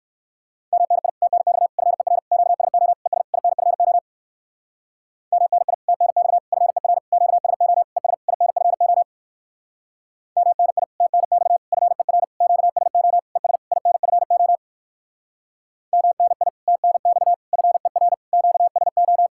Morse code, heard as a keyed tone, about 67 wpm, 700 Hz. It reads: MNI TNX FER QSO ES INFO MNI TNX FER QSO ES INFO MNI TNX FER QSO ES INFO MNI TNX FER QSO